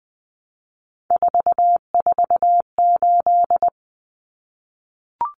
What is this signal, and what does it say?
Morse code, heard as a keyed tone, about 20 wpm, 700 Hz. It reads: 448
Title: Numbers - 3-Digit 20wpm